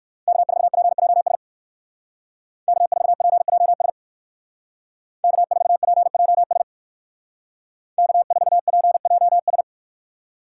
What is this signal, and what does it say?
Morse code, heard as a keyed tone, about 54 wpm, 700 Hz. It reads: K4PJS K4PJS K4PJS K4PJS
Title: Speed Racing - USA 1x3 Call Signs 45wpm